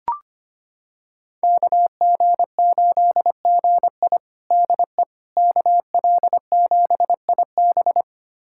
Morse code, heard as a keyed tone, about 25 wpm, 700 Hz. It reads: KG8GI DE KL7IB
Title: USA Callsign DE Callsign 25wpm